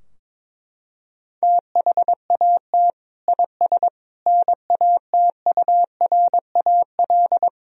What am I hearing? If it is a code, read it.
Morse code, heard as a keyed tone, about 22 wpm, 700 Hz. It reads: THAT IS NATURAL
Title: Sentences from Top 1000 Words - Review Part 4 - 22wpm